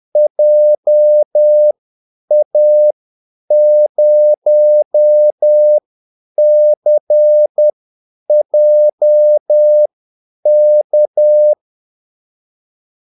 Morse code, heard as a keyed tone, about 10 wpm, spaced as for 8 wpm, 600 Hz. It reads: JA0CJK
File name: Callsigns-10-8wpm